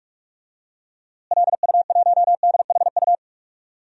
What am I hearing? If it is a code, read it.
Morse code, heard as a keyed tone, about 41 wpm, 700 Hz. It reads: LU1DHU
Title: DXCC-100 Call Signs - 6 Characters - Repeated 2x - 45wpm